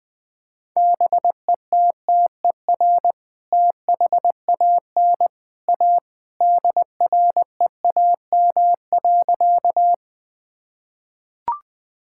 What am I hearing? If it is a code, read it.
Morse code, heard as a keyed tone, about 20 wpm, 700 Hz. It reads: BETTER THAN A DREAM.
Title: Sentences with 2-Word Phrases - Spoken Beginning - 20wpm